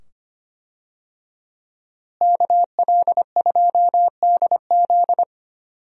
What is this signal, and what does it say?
Morse code, heard as a keyed tone, about 25 wpm, 700 Hz. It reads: KL2DZ
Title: USA Call Signs - 5 Characters - Repeated 2x - 25wpm